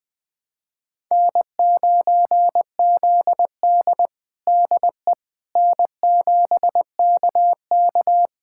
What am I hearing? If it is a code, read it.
Morse code, heard as a keyed tone, about 21 wpm, 700 Hz. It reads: N9ZD DE N7KK
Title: USA Callsign DE Callsign - 4 Characters - 20wpm